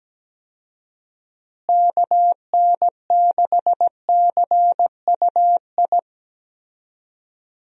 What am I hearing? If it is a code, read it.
Morse code, heard as a keyed tone, about 17 wpm, 700 Hz. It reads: KN6CUI
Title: USA Call Signs - 6 Characters - Repeated 3x - 17wpm